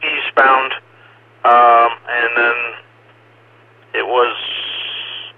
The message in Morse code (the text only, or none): none